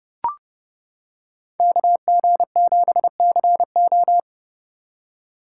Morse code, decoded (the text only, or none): KG7CO